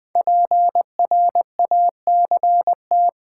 Morse code, heard as a keyed tone, about 20 wpm, 700 Hz. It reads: PRACT
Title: Build A Word - 5 Letters At A Time - 12-Letter Words - 20wpm